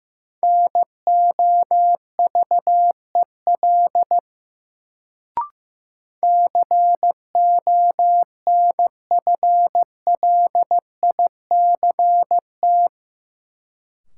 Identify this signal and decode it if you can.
Morse code, heard as a keyed tone, about 16 wpm, 700 Hz. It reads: NOVEL CONFLICT
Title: Sentences from Top 1300 Words - New Word Drill - 15wpm